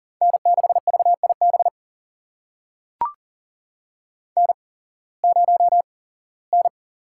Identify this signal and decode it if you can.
Morse code, heard as a keyed tone, about 40 wpm, spaced as for 21 wpm, 700 Hz. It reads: N6VIB N0N
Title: USA Call Signs - 5 Characters - 8x Character Spacing - 40wpm